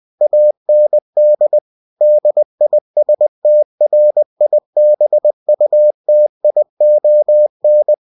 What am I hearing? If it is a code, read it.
Morse code, heard as a keyed tone, about 20 wpm, 600 Hz. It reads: AND DISTRIBUTION
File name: cw-books / winnie-the-pooh / winnie-20wpm-page-0120